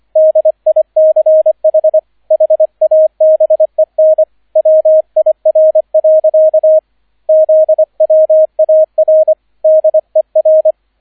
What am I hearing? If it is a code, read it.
Morse code, heard as a keyed tone, about 25 wpm, 630 Hz, with radio noise. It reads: DICH HABEN WIR. ZWAR DER